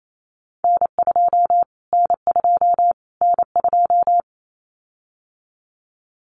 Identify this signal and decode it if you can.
Morse code, heard as a keyed tone, about 28 wpm, 700 Hz. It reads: N2 N2 N2
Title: Callsign Prefixes for Top 25 DXCC - Repeated 3x - 28wpm